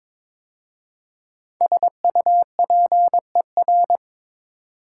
Morse code, heard as a keyed tone, about 22 wpm, 700 Hz. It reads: SUPER
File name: 5 Letter Words - First 3 Letters Given - 22wpm